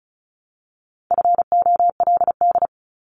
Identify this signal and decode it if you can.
Morse code, heard as a keyed tone, about 36 wpm, 700 Hz. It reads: FOLD
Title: Multi-letter Phonemes Repeated 3x 35wpm